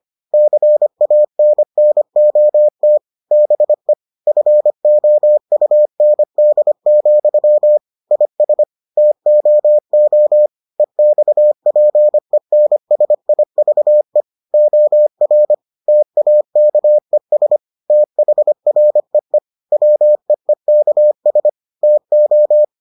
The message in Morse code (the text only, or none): CANNOT BE FOUND, IS TOO EXPENSIVE OR TAKES THREE WEEKS TO